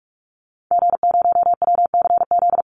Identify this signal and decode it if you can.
Morse code, heard as a keyed tone, about 44 wpm, 700 Hz. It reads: G0WCZ
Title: Call Signs - 45wpm